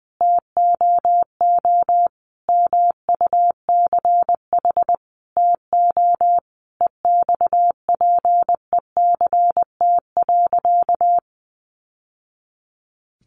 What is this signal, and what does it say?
Morse code, heard as a keyed tone, about 20 wpm, 700 Hz. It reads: TOO MUCH TO EXPECT.
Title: Sentences from Top 1100 Words 20wpm